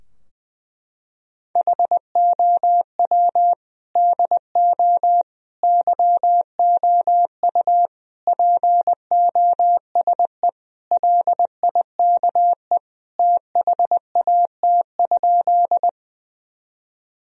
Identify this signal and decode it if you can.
Morse code, heard as a keyed tone, about 20 wpm, 700 Hz. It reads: HOW DO YOU POSE LIKE THAT?
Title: Sentences from Top 1000 Words - Review Part 2 - 20wpm